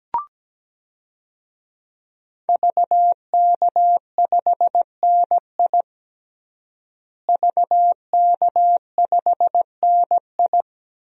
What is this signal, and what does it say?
Morse code, heard as a keyed tone, about 17 wpm, 700 Hz. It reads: VK5NI VK5NI
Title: DXCC-100 Call Signs - 5 Characters - Repeated 2x - 17wpm